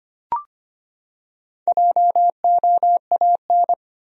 Morse code, heard as a keyed tone, about 25 wpm, 700 Hz. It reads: JOAN
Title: USA Common Names - 25wpm